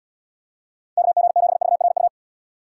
Morse code, heard as a keyed tone, about 77 wpm, 700 Hz. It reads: KG7VRU